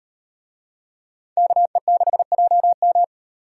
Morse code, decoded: KE6JM

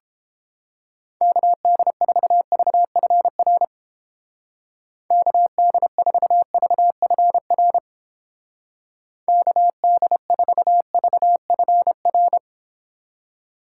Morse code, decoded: KD4VFR KD4VFR KD4VFR

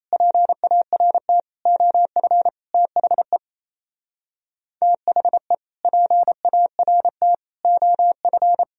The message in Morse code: PART OF THE THE PART OF

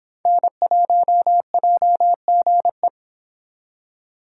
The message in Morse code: N1JGE

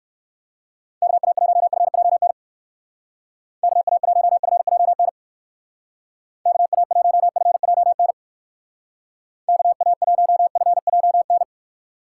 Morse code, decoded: KA1FJN KA1FJN KA1FJN KA1FJN

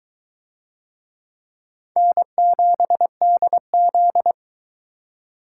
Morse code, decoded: N7DZ

